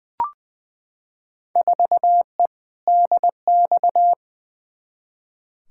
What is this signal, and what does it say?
Morse code, heard as a keyed tone, about 20 wpm, 700 Hz. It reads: 4E DX
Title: Field Day Exchanges Encore - 20wpm